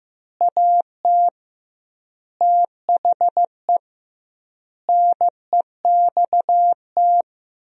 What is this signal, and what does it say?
Morse code, heard as a keyed tone, about 15 wpm, 700 Hz. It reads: AT THE NEXT
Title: 2x Word-Spacing - Sentences from Top 400 Words - 15wpm